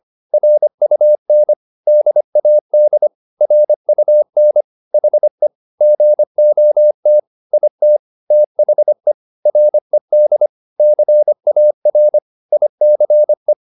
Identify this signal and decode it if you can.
Morse code, heard as a keyed tone, about 25 wpm, 600 Hz. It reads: RUN DAD RUN HE GOT IT THE RED CAR ICE